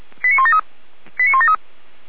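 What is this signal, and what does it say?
Five-tone selective call (ZVEI), heard as quick runs of five tones, tones of about 70 ms. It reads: 89173 89173
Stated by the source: ZVEI five-tone call 89235 received via SDR, from the feuerwehr-orlamuende (Orlamünde fire brigade) audio server.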